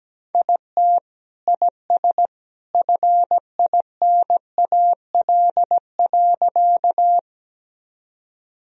Morse code, heard as a keyed tone, about 17 wpm, 700 Hz. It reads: IT IS FINAL.
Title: Sentences from Top 1000 Words - Review Part 2 - 17wpm